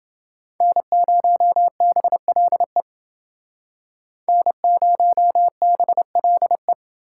Morse code, heard as a keyed tone, about 29 wpm, 700 Hz. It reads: N0BLE N0BLE